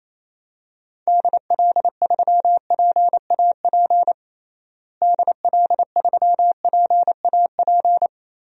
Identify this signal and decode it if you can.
Morse code, heard as a keyed tone, about 27 wpm, 700 Hz. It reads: DL3PAP DL3PAP
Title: DXCC-100 Call Signs - 6 Characters - Repeated 2x - 28wpm